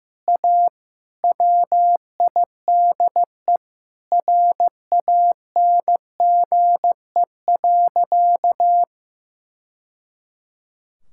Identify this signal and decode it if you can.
Morse code, heard as a keyed tone, about 15 wpm, 700 Hz. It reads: A WIDE RANGE.
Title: Sentences from Top 1000 Words - Review Part 6 - 15wpm